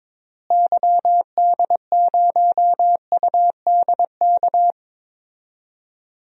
Morse code, heard as a keyed tone, about 21 wpm, 700 Hz. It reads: YD0UDK
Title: DXCC-100 Call Signs - 6 Characters - Repeated 3x - 22wpm